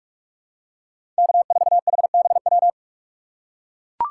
Morse code, heard as a keyed tone, about 41 wpm, 700 Hz. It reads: K4HBW
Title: USA Call Signs - 5 Characters - 2x Character Spacing - 45wpm